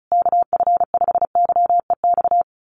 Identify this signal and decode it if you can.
Morse code, heard as a keyed tone, about 34 wpm, 700 Hz. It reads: KF5YEX
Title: USA Call Signs Repeated 3x 35wpm